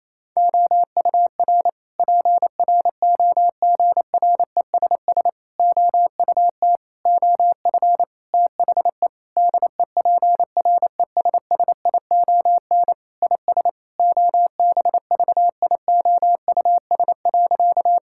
Morse code, heard as a keyed tone, about 28 wpm, 700 Hz. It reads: OUR PROGRESS OUT OF THE DEPRESSION IS OBVIOUS.
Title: Sentences from Top 1600 Words 28wpm